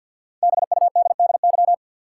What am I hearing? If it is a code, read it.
Morse code, heard as a keyed tone, about 51 wpm, 700 Hz. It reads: BUDDY
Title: Five Letter Words - 50wpm